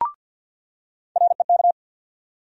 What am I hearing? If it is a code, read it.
Morse code, heard as a keyed tone, about 47 wpm, 700 Hz. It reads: REK
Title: USA Call Sign Postfixes - 3 Letters 50wpm